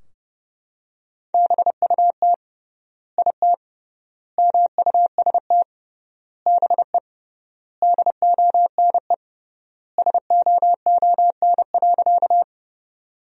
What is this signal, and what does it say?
Morse code, heard as a keyed tone, about 30 wpm, 700 Hz. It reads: BUT IT MUST BE DONE SOON.
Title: 3x Word-Spacing - Sentences from Top 500 Words - 30wpm